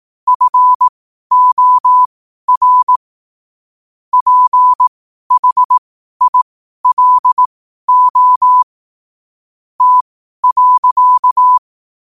Morse code, heard as a keyed tone, about 18 wpm, spaced as for 12 wpm, 1000 Hz. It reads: FOR PHILO T.
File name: qotdfiles / 12wpm